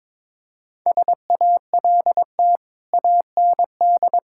Morse code, heard as a keyed tone, about 22 wpm, 700 Hz. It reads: SALT AND
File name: practice / 22wpm / Build A Phrase - 5-Word Phrases - 22wpm